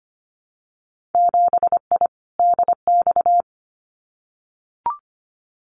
Morse code, heard as a keyed tone, about 25 wpm, 700 Hz. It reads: 7I DX
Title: Winter Field Day Exchanges Encore 25wpm